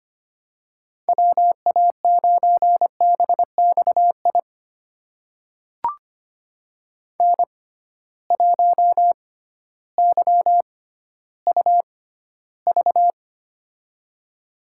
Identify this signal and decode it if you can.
Morse code, heard as a keyed tone, about 25 wpm, spaced as for 14 wpm, 700 Hz. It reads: WA9BXI N1YUV
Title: USA Call Signs - 6x Character Spacing - 25wpm